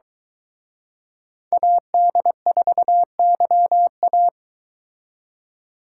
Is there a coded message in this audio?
Yes: Morse code from 1.5 to 4.3 s, heard as a keyed tone, about 23 wpm, 700 Hz. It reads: AD4YA